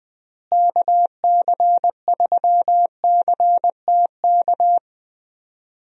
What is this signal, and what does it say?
Morse code, heard as a keyed tone, about 20 wpm, 700 Hz. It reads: KC3CTK
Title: USA Call Signs - 6 Characters - Repeated 3x - 20wpm